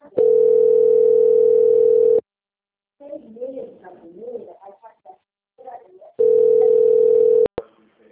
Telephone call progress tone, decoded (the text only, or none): ringback tone